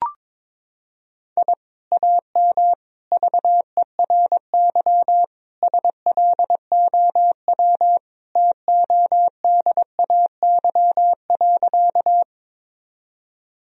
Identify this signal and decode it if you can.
Morse code, heard as a keyed tone, about 22 wpm, 700 Hz. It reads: I AM VERY SLOW TODAY.
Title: Sentences from Top 1000 Words - Review Part 2 - 22wpm